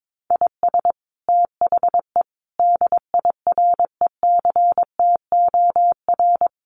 Morse code, heard as a keyed tone, about 22 wpm, 700 Hz. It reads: IS THE DIRECTOR
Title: Sets of 5 Words Encore2 22wpm